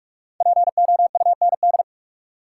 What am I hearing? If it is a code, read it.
Morse code, heard as a keyed tone, about 43 wpm, 700 Hz. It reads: POUND
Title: Sentences from Top 1000 Words - Review Part 2 - New Word Drill - 45wpm